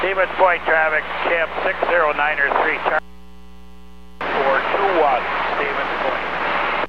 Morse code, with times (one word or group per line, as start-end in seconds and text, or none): none